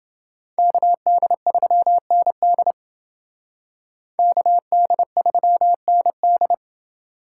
Morse code, heard as a keyed tone, about 28 wpm, 700 Hz. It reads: KD3ND KD3ND